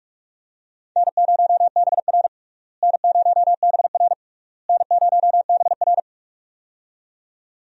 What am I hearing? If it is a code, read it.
Morse code, heard as a keyed tone, about 44 wpm, 700 Hz. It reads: N0BR N0BR N0BR
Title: USA Call Signs - 4 Characters - Repeated 3x - 45wpm